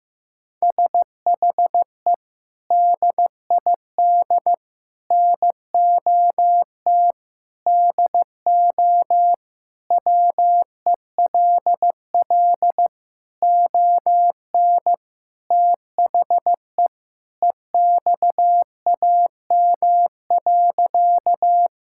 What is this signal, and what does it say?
Morse code, heard as a keyed tone, about 15 wpm, 700 Hz. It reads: SHE DID NOT DO WELL ON THE EXAM.
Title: Sentences from Top 1000 Words - Review Part 5 - 15wpm